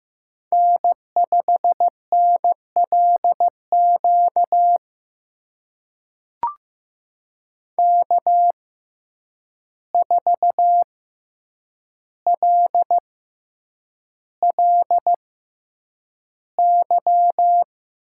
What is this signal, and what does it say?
Morse code, heard as a keyed tone, about 15 wpm, spaced as for 8 wpm, 700 Hz. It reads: N5NLQ K4LLY
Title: USA Call Signs - 5 Characters - 6x Character Spacing - 15wpm